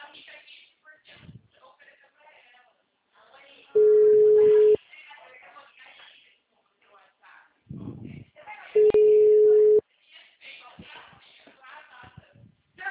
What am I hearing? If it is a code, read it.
Ringback tone (Europe, 425 Hz), with speech.